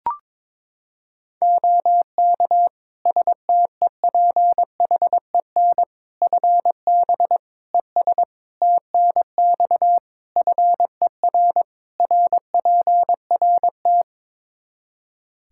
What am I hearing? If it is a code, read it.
Morse code, heard as a keyed tone, about 22 wpm, 700 Hz. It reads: OK STEPHEN FB ES TNX FER RPRT
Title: K1USN SST QSO Practice 22wpm